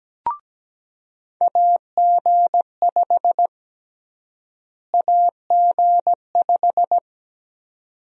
Morse code, heard as a keyed tone, about 17 wpm, 700 Hz. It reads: AG5 AG5
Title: USA Call Sign Prefixes - Repeated 2x - 17wpm